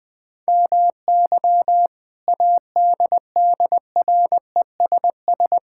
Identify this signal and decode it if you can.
Morse code, heard as a keyed tone, about 21 wpm, 700 Hz. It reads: MY ADDRESS
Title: Sentences from Top 1000 Words - Review Part 6 - 20wpm